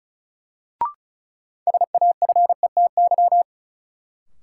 Morse code, heard as a keyed tone, about 36 wpm, 700 Hz. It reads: SAFETY
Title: Sentences from Top 700 Words - New Word Drill - 35wpm